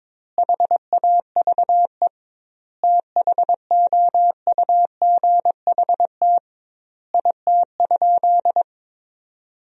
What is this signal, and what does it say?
Morse code, heard as a keyed tone, about 22 wpm, 700 Hz. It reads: HAVE THOUGHT IT?
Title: 2x Word-Spacing - Sentences from Top 300 Words - 22wpm